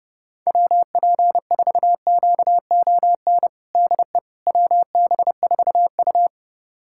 Morse code, heard as a keyed tone, about 30 wpm, 700 Hz. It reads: WP4QON DE WB4U
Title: USA Callsign DE Callsign 30wpm